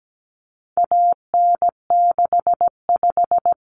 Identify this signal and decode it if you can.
Morse code, heard as a keyed tone, about 17 wpm, 700 Hz. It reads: AN65
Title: Common Cut Numbers - 4-Digit 17wpm